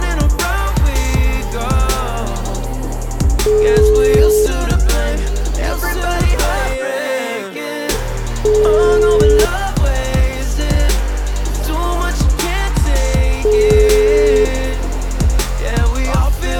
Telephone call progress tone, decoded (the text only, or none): ringback tone